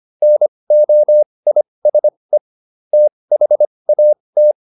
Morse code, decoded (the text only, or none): NOISE THAT